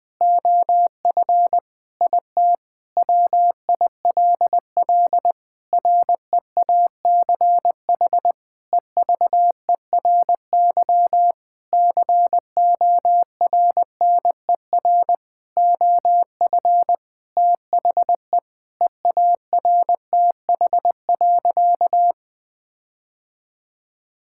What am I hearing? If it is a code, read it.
Morse code, heard as a keyed tone, about 20 wpm, 700 Hz. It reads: OF IT WILL REACH EVERY CORNER OF THE EARTH.
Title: Sentences from Top 1600 Words 20wpm